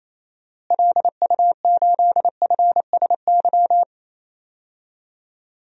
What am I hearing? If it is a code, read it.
Morse code, heard as a keyed tone, about 28 wpm, 700 Hz. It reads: LU8FSY
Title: DXCC-100 Call Signs - 6 Characters - Repeated 2x - 28wpm